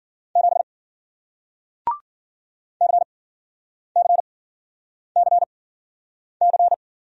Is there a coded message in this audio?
Yes: Morse code from 0.4 to 6.7 s, heard as a keyed tone, about 53 wpm, spaced as for 9 wpm, 700 Hz. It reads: 7 CCCC